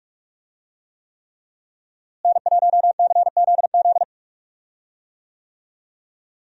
Morse code, heard as a keyed tone, about 46 wpm, 700 Hz. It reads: N1CZZ
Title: USA Call Signs - 5 Characters - Repeated 3x - 45wpm